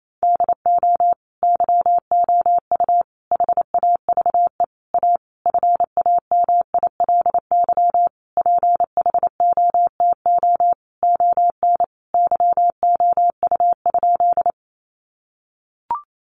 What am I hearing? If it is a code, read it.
Morse code, heard as a keyed tone, about 28 wpm, 700 Hz. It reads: DO YOU HAVE A FAMILY PHOTO ON YOU?
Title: Sentences from Top 1200 Words 28wpm